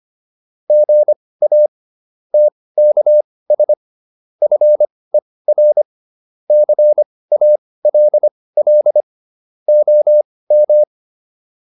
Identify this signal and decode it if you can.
Morse code, heard as a keyed tone, about 25 wpm, spaced as for 18 wpm, 600 Hz. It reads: GA TKS FER CALL OM
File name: QSO409 18